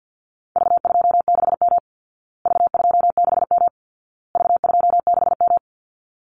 Morse code, heard as a keyed tone, about 50 wpm, 700 Hz. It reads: V26M V26M V26M
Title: Call Signs Repeated 2x - 50wpm